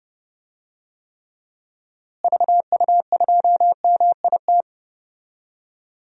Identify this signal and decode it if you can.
Morse code, heard as a keyed tone, about 30 wpm, 700 Hz. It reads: VU2MIT
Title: DXCC-100 Call Signs - 6 Characters - Repeated 3x - 30wpm